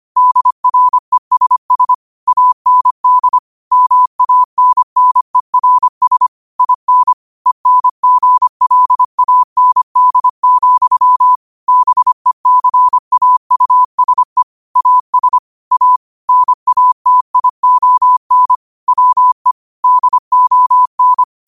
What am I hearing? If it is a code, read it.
Morse code, heard as a keyed tone, about 25 wpm, 1000 Hz. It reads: DRESS AND MANNERS IN ENGLAND, BECAUSE AS A NATION WE DON
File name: qotdfiles / 25wpm